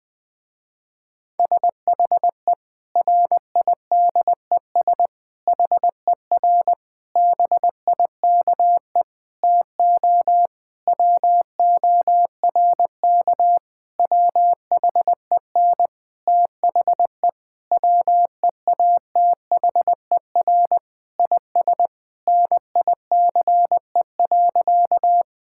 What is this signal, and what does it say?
Morse code, heard as a keyed tone, about 21 wpm, 700 Hz. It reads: SHE RIDES HER BIKE TO WORK WHEN THE WEATHER IS NICE.
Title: Difficult Sentences with Hints - 20wpm